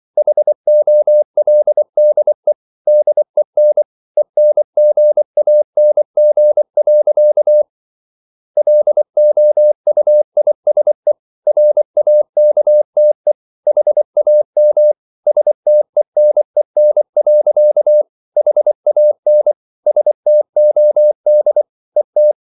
Morse code, heard as a keyed tone, about 24 wpm, 600 Hz. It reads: HOLDE DEN ENGANG. LOUISE RAKTE HAM STENEN. HAN STOD ET
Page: Ferie 24 wpm